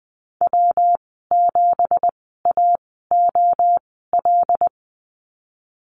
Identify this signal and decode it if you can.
Morse code, heard as a keyed tone, about 20 wpm, spaced as for 14 wpm, 700 Hz. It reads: W7AOL
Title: 213 Lesson 34g - Number 8 - Call Signs 20wpm